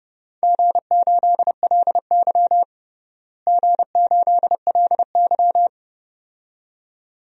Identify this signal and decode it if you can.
Morse code, heard as a keyed tone, about 31 wpm, 700 Hz. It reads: G8LY G8LY
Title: DXCC-100 Call Signs - 4 Characters - Repeated 2x - 30wpm